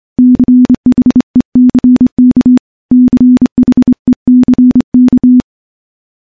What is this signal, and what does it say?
Morse code, heard as a keyed tone, about 26 wpm, 250 Hz. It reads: CHECK CHECK
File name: morse-training-25WPM